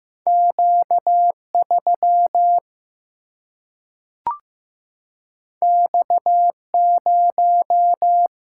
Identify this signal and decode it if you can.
Morse code, heard as a keyed tone, about 16 wpm, 700 Hz. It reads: Q3 X0
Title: Single-Letter Single-Number 15wpm